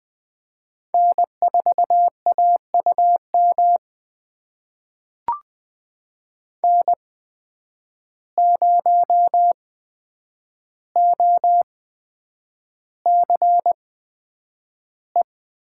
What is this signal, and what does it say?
Morse code, heard as a keyed tone, about 20 wpm, spaced as for 9 wpm, 700 Hz. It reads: N4AUM N0OCE
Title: USA Call Signs - 5 Characters - 8x Character Spacing - 20wpm